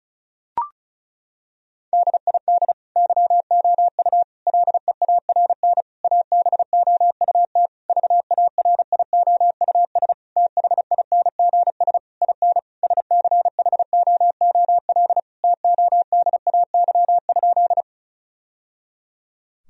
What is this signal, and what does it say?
Morse code, heard as a keyed tone, about 34 wpm, 700 Hz. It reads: DID YOU LEARN ABOUT VARIOUS THINGS IN SCHOOL TODAY?
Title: Sentences from Top 1000 Words - Review Part 4 - 35wpm